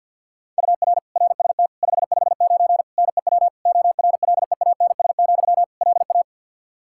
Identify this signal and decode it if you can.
Morse code, heard as a keyed tone, about 52 wpm, 700 Hz. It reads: UR RST 559 NEW ORLEANS, LA
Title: Practice QSO - Exchange 1 - Beginning - 50wpm